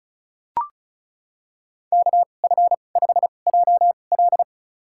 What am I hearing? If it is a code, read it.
Morse code, heard as a keyed tone, about 34 wpm, spaced as for 25 wpm, 700 Hz. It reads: KF5JL
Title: USA Call Signs - 5 Characters - 2x Character Spacing - 35wpm